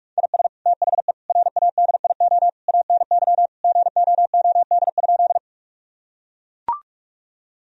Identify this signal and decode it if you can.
Morse code, heard as a keyed tone, about 41 wpm, 700 Hz. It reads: IS THE RADIO ANY GOOD?
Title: Sentences from Top 1000 Words - Review Part 3 - 45wpm